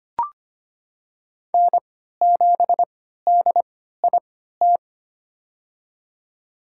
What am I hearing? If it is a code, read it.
Morse code, heard as a keyed tone, about 25 wpm, spaced as for 14 wpm, 700 Hz. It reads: N7DIT
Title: USA Call Signs - 5 Characters - 3x Character Spacing - 25wpm